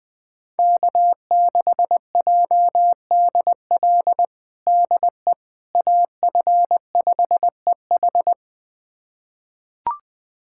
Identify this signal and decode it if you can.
Morse code, heard as a keyed tone, about 20 wpm, 700 Hz. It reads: K6JDL DE AF5EH
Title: USA Callsign DE Callsign - 5 Characters - 20wpm